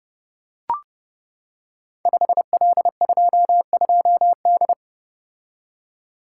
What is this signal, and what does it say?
Morse code, heard as a keyed tone, about 28 wpm, 700 Hz. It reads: 5L22D
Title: Code Groups - 5 Letter:Numbers - 30wpm